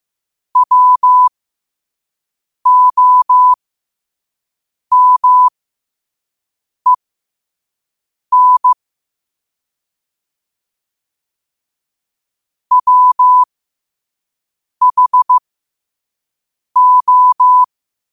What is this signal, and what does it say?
Morse code, heard as a keyed tone, about 15 wpm, spaced as for 5 wpm, 1000 Hz. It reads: WOMEN WHO
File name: qotdfiles / 5wpm